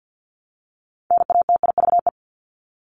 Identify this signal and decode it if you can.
Morse code, heard as a keyed tone, about 50 wpm, 700 Hz. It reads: NATIVE
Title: Top 1751-2000 Words 50wpm